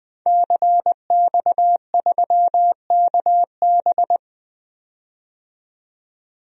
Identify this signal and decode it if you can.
Morse code, heard as a keyed tone, about 20 wpm, 700 Hz. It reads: CX3KB
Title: DXCC-100 Call Signs - 5 Characters - 20wpm